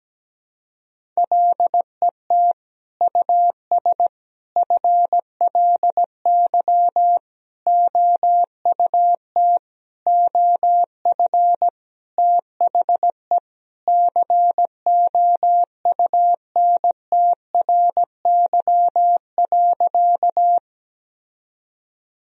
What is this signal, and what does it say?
Morse code, heard as a keyed tone, about 17 wpm, 700 Hz. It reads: LET US FLY OUT OF THE COUNTRY.
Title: Sentences from Top 1000 Words - Review Part 2 - 17wpm